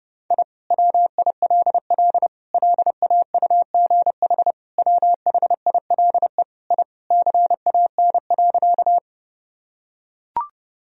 Morse code, heard as a keyed tone, about 30 wpm, 700 Hz. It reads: I WILL LAUGH WHILE I CAN.
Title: Sentences from Top 1000 Words - Review Part 1 - 30wpm